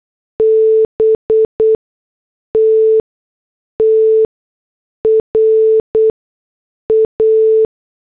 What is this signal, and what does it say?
Morse code, heard as a keyed tone, about 8 wpm, spaced as for 6 wpm, 440 Hz. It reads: BT TRA